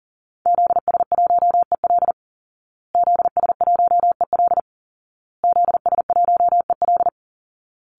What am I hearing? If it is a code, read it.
Morse code, heard as a keyed tone, about 40 wpm, 700 Hz. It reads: ZS1EL ZS1EL ZS1EL
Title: Call Signs Repeated 2x - 40wpm